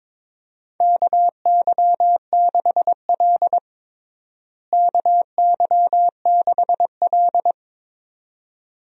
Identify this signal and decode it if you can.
Morse code, heard as a keyed tone, about 22 wpm, 700 Hz. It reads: KY6L KY6L
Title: USA Call Signs - 4 Characters - Repeated 2x - 22wpm